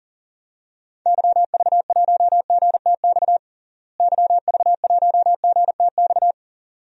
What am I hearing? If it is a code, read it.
Morse code, heard as a keyed tone, about 40 wpm, 700 Hz. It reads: YV1GTX YV1GTX
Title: DXCC-100 Call Signs - 6 Characters - Repeated 2x - 40wpm